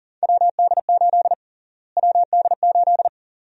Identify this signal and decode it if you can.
Morse code, heard as a keyed tone, about 40 wpm, 700 Hz. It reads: WD8 WD8
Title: USA Call Sign Prefixes - Repeated 2x - 40wpm